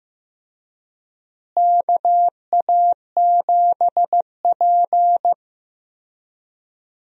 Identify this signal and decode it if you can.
Morse code, heard as a keyed tone, about 15 wpm, 700 Hz. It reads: KA7P
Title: USA Call Signs - 4 Characters - Repeated 3x - 15wpm